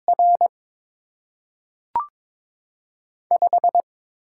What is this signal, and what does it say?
Morse code, heard as a keyed tone, about 22 wpm, 700 Hz. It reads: R 5
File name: Top 20 QSO Elements - 8x Character Spacing - 22wpm